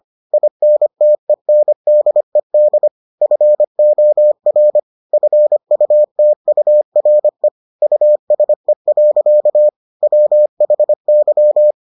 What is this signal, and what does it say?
Morse code, heard as a keyed tone, about 25 wpm, 600 Hz. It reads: INTENDED FOR FUTURE USE. WHY